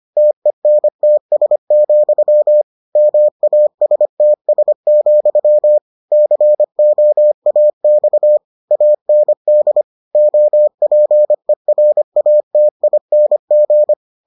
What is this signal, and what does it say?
Morse code, heard as a keyed tone, about 25 wpm, 600 Hz. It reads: TENTS, MASTS, COAX AND OPERATING